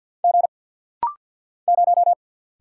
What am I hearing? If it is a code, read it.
Morse code, heard as a keyed tone, about 49 wpm, 700 Hz. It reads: G 0
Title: Single Letter-Number - Rapid-Fire 50wpm